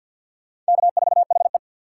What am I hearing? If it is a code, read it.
Morse code, heard as a keyed tone, about 63 wpm, 700 Hz. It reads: K4HE